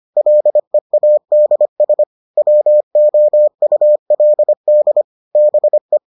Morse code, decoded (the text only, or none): LEADS WOULD BE